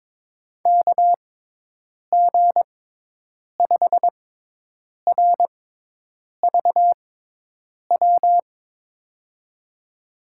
Morse code, heard as a keyed tone, about 21 wpm, spaced as for 8 wpm, 700 Hz. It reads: KG5RVW
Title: USA Call Signs - 6 Characters - 6x Character Spacing - 22wpm